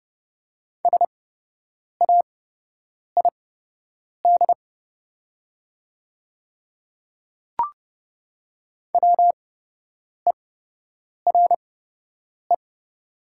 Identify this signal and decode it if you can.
Morse code, heard as a keyed tone, about 30 wpm, spaced as for 8 wpm, 700 Hz. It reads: SAID WERE
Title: 8x Character Spacing - Top 200 Words - 30wpm